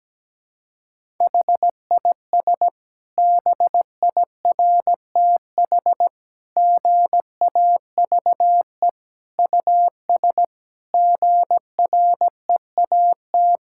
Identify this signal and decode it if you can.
Morse code, heard as a keyed tone, about 17 wpm, 700 Hz. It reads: HIS BIRTH GAVE US GREAT
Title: Sentences from Top 1000 Words - Review Part 5 - 17wpm